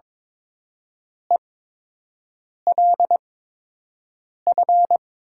Morse code, heard as a keyed tone, about 22 wpm, spaced as for 6 wpm, 700 Hz. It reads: ELF
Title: 8x Character Spacing - Top 500 Words - 22wpm